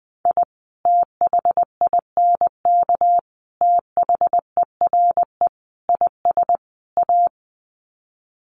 Morse code, heard as a keyed tone, about 20 wpm, 700 Hz. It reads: I THINK THERE IS A
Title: Sets of 5 Words 20wpm